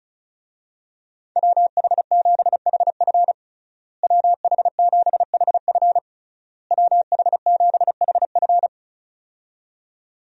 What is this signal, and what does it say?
Morse code, heard as a keyed tone, about 36 wpm, 700 Hz. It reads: WH7HF WH7HF WH7HF
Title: USA Call Signs - 5 Characters - Repeated 3x - 35wpm